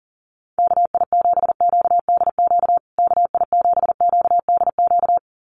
Morse code, heard as a keyed tone, about 40 wpm, 700 Hz. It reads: KI7QDQ KI7QDQ
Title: USA Call Signs Repeated 2x 40wpm